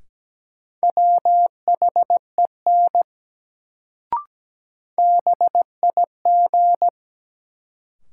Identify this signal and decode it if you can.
Morse code, heard as a keyed tone, about 17 wpm, 700 Hz. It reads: WHEN BIG
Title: Sets of 2 Words - New Word Drill - 17wpm